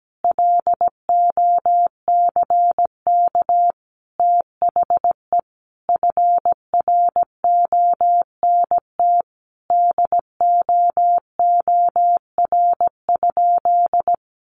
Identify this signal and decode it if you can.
Morse code, heard as a keyed tone, about 17 wpm, 700 Hz. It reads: LOCK THE FRONT DOOR?
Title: Sentences from Top 1300 Words 17wpm